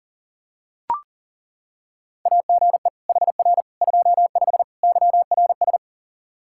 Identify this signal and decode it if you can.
Morse code, heard as a keyed tone, about 40 wpm, 700 Hz. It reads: AGE HR 25 YRS
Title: K1USN SST QSO Practice 40wpm